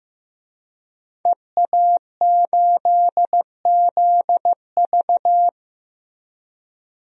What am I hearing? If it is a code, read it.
Morse code, heard as a keyed tone, about 15 wpm, 700 Hz. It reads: EA8ZV
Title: DXCC-100 Call Signs - 5 Characters - 15wpm